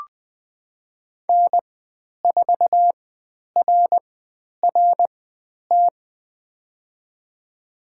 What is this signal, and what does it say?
Morse code, heard as a keyed tone, about 20 wpm, spaced as for 10 wpm, 700 Hz. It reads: N4RRT
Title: 095 Lesson 16g - Letter c - Call Signs 20wpm